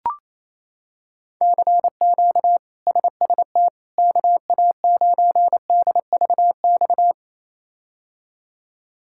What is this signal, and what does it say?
Morse code, heard as a keyed tone, about 28 wpm, 700 Hz. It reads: CQ SST KA9DVX
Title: K1USN SST QSO Practice - 28wpm